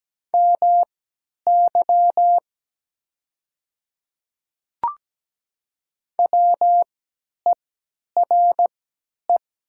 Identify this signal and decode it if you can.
Morse code, heard as a keyed tone, about 17 wpm, spaced as for 10 wpm, 700 Hz. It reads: MY WERE
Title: 3x Character Spacing - Top 100 Words - 17wpm